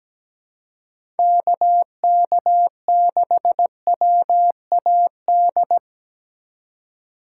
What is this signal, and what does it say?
Morse code, heard as a keyed tone, about 17 wpm, 700 Hz. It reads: KK6WAD
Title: USA Call Signs - 6 Characters - 2x Character Spacing - 17wpm